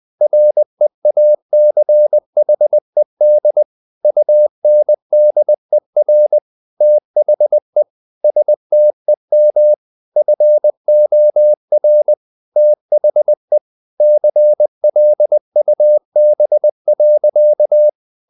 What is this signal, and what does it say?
Morse code, heard as a keyed tone, about 20 wpm, 600 Hz. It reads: REACHED UNDER THE STEM FOR THE CLUB.
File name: the old man and the sea part 2 20wpm